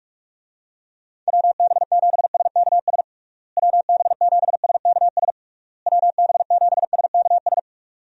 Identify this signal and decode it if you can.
Morse code, heard as a keyed tone, about 47 wpm, 700 Hz. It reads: WB7SKS WB7SKS WB7SKS
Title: USA Call Signs - 6 Characters - Repeated 3x - 45wpm